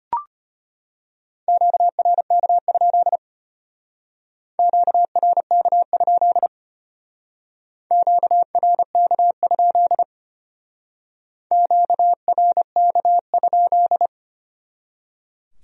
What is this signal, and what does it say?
Morse code, heard as a keyed tone, about 32 wpm, 700 Hz. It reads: QRK? QRK? QRK? QRK?